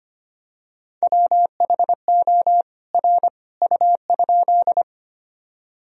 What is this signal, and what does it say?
Morse code, heard as a keyed tone, about 25 wpm, 700 Hz. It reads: WHO R U?
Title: Sentences from Top 100 Words and Abbreviations 25wpm